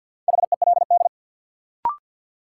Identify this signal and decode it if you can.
Morse code, heard as a keyed tone, about 52 wpm, 700 Hz. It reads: HELD
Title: Sentences from Top 1600 Words - New Word Drill - 50wpm